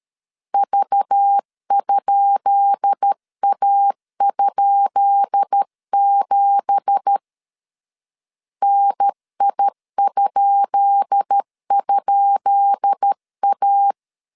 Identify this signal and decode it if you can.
Morse code, heard as a keyed tone, about 13 wpm, 800 Hz. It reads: V?A?7 NI??A
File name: LEZIONI / 10 SIGNS / 10B INT 12wpm 2414